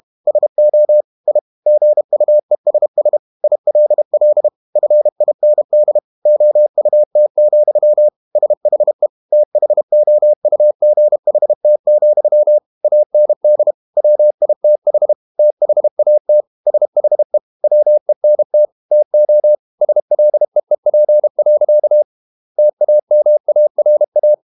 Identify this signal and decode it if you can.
Morse code, heard as a keyed tone, about 32 wpm, 600 Hz. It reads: SO I GUESS ILL FIND OUT, SHE THOUGHT, AND WITH THAT SHE WENT TO SLEEP. TAMARA
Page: Never 31wpm | CW med Gnister